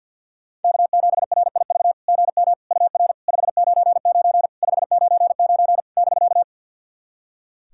Morse code, heard as a keyed tone, about 46 wpm, 690 Hz. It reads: K7RIV GM UR 599 599 <BK>
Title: SOTA QSO Practice 50wpm